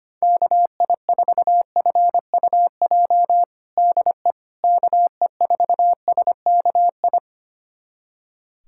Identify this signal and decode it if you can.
Morse code, heard as a keyed tone, about 25 wpm, 700 Hz. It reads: KI4FUJ DE KE4SKI
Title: USA Callsign DE Callsign - 6 Characters - 25wpm